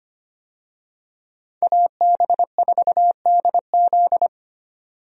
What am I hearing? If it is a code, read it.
Morse code, heard as a keyed tone, about 25 wpm, 700 Hz. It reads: AB4DZ
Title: USA Call Signs - 5 Characters - 25wpm